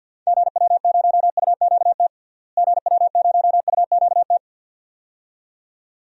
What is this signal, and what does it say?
Morse code, heard as a keyed tone, about 47 wpm, 700 Hz. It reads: GW0UQT GW0UQT
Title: DXCC-100 Call Signs - 6 Characters - Repeated 2x - 50wpm